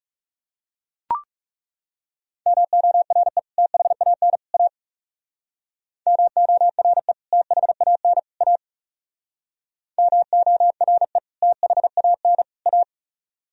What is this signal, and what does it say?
Morse code, heard as a keyed tone, about 35 wpm, 700 Hz. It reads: MORE THAN A MORE THAN A MORE THAN A
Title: Speed-Racing - Sets of 3 Words 30wpm